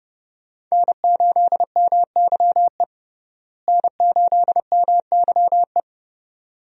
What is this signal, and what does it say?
Morse code, heard as a keyed tone, about 30 wpm, 700 Hz. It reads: N8MYE N8MYE
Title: USA Call Signs - 5 Characters - Repeated 2x - 30wpm